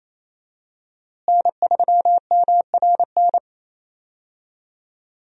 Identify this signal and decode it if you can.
Morse code, heard as a keyed tone, about 28 wpm, 700 Hz. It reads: N3MRN
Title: USA Call Signs - 5 Characters - Repeated 3x - 28wpm